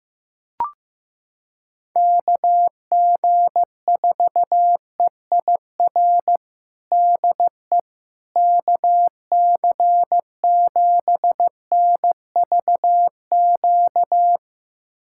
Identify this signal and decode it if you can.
Morse code, heard as a keyed tone, about 15 wpm, 700 Hz. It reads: KG4EIR DE KC7NVQ
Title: USA Callsign DE Callsign 15wpm